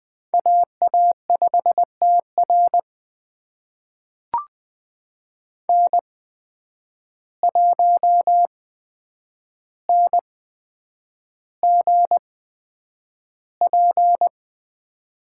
Morse code, heard as a keyed tone, about 20 wpm, spaced as for 9 wpm, 700 Hz. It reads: AA5TR N1NGP
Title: USA Call Signs - 8x Character Spacing - 20wpm